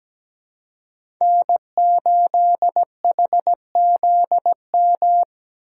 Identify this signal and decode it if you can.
Morse code, heard as a keyed tone, about 17 wpm, 700 Hz. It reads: N8HZM
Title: K1USN SST QSO Practice - 17wpm